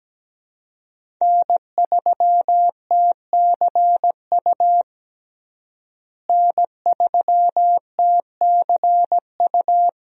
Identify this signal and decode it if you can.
Morse code, heard as a keyed tone, about 17 wpm, 700 Hz. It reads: N3TCU N3TCU
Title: USA Call Signs - 5 Characters - Repeated 2x - 17wpm